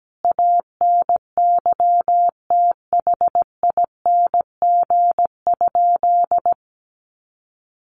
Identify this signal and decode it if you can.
Morse code, heard as a keyed tone, about 17 wpm, 700 Hz. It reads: ANYTHING?
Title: Sentences from Top 1000 Words 17wpm